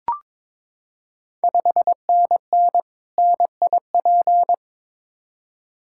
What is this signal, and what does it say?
Morse code, heard as a keyed tone, about 23 wpm, 690 Hz. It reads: 5NN NIP
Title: Ontario QSO Party Exchange 22wpm